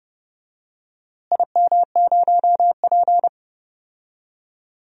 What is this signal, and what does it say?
Morse code, heard as a keyed tone, about 30 wpm, 700 Hz. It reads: IM0P
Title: DXCC-100 Call Signs - 4 Characters - Repeated 3x - 30wpm